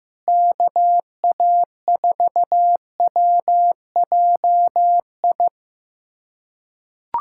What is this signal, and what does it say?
Morse code, heard as a keyed tone, about 16 wpm, 700 Hz. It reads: KA4WJI
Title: USA Callsign DE Callsign - 6 Characters - 15wpm